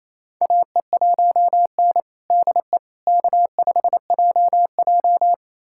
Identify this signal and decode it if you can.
Morse code, heard as a keyed tone, about 28 wpm, 700 Hz. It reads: AE1N DE K5JJ
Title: USA Callsign DE Callsign - 4 Characters - 28wpm